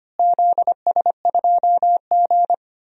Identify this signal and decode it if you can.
Morse code, heard as a keyed tone, about 25 wpm, 700 Hz. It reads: ZS2G
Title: DXCC-100 Call Signs - 4 Characters - Repeated 2x - 25wpm